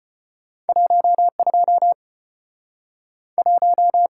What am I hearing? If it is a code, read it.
Morse code, heard as a keyed tone, about 34 wpm, 700 Hz. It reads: 12 1